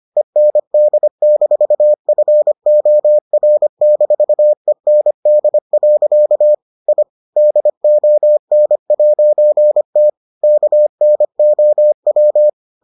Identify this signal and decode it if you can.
Morse code, heard as a keyed tone, about 25 wpm, 600 Hz. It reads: END-FOR-END. I DON'T KNOW